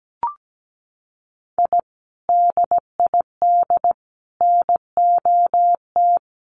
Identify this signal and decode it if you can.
Morse code, heard as a keyed tone, about 17 wpm, 700 Hz. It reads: I DID NOT
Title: Sets of 5 Words 17wpm